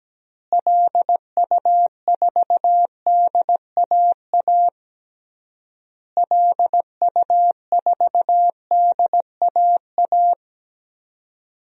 Morse code, heard as a keyed tone, about 17 wpm, 700 Hz. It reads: LU4DAA LU4DAA
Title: DXCC-100 Call Signs - 6 Characters - Repeated 3x - 17wpm